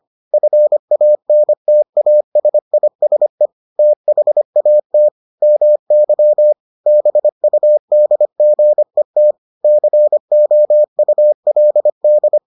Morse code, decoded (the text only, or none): FANTASISE THAT MY BUDGET COULD